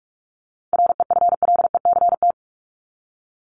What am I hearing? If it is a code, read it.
Morse code, heard as a keyed tone, about 44 wpm, 700 Hz. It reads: REFLECT
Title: Top 1251-1500 Words 45wpm